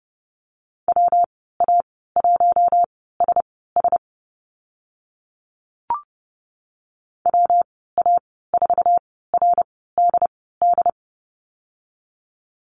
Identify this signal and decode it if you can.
Morse code, heard as a keyed tone, about 30 wpm, spaced as for 17 wpm, 700 Hz. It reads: WA1SS WA4RDD
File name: Lesson 19g - Letter w - Call Signs 30wpm